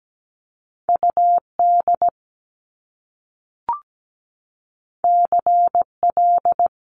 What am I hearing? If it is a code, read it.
Morse code, heard as a keyed tone, about 17 wpm, 700 Hz. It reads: UD CL
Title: Top Two Letter Combinations in Words 17wpm